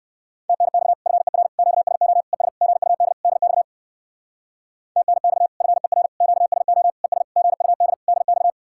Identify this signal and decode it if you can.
Morse code, heard as a keyed tone, about 58 wpm, 700 Hz. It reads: TNX FER QSO ES GUD DX TNX FER QSO ES GUD DX